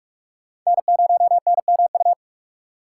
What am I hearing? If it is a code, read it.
Morse code, heard as a keyed tone, about 46 wpm, 700 Hz. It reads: N0NMU
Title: USA Call Signs - 6x Character Spacing - 45wpm